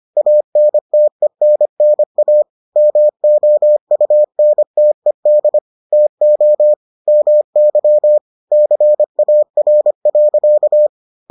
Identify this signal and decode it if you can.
Morse code, heard as a keyed tone, about 25 wpm, 600 Hz. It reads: ANTENNA MOUNTED TO MY CAR.